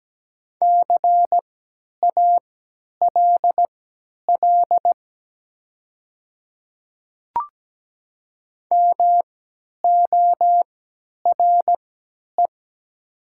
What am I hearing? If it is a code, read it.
Morse code, heard as a keyed tone, about 17 wpm, spaced as for 10 wpm, 700 Hz. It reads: CALL MORE
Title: 3x Character Spacing - Top 200 Words - 17wpm